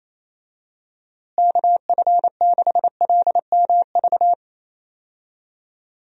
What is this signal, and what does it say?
Morse code, heard as a keyed tone, about 28 wpm, 700 Hz. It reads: KF6LMV
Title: USA Call Signs - 6 Characters - Repeated 3x - 28wpm